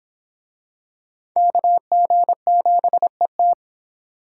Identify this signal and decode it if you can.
Morse code, heard as a keyed tone, about 25 wpm, 700 Hz. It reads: KG7ET